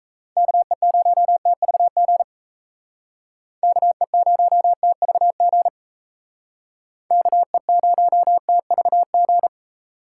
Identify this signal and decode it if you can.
Morse code, heard as a keyed tone, about 36 wpm, 700 Hz. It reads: KE0TVG KE0TVG KE0TVG